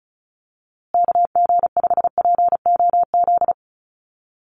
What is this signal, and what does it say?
Morse code, heard as a keyed tone, about 34 wpm, 700 Hz. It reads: KG5POZ
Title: USA Call Signs Repeated 2x 35wpm